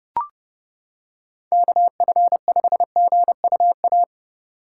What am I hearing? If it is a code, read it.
Morse code, heard as a keyed tone, about 29 wpm, 700 Hz. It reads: KF5GUA